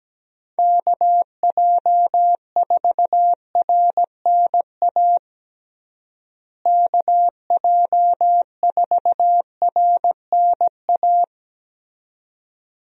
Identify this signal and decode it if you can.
Morse code, heard as a keyed tone, about 17 wpm, 700 Hz. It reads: KJ4RNA KJ4RNA
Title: USA Call Signs - 6 Characters - Repeated 2x - 17wpm